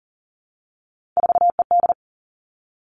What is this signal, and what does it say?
Morse code, heard as a keyed tone, about 38 wpm, 700 Hz. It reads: 4ED
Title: Three Challenging Letter-Number Combinations 40wpm